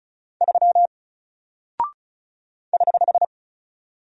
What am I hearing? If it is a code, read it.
Morse code, heard as a keyed tone, about 36 wpm, 700 Hz. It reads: 3 <HH>
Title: Single Character (Letter:Number:Punctuation:Prosign) Alt 35wpm